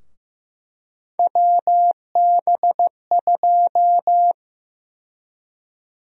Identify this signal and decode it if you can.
Morse code, heard as a keyed tone, about 16 wpm, 700 Hz. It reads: WB2
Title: USA Call Sign Prefixes - Repeated 2x - 15wpm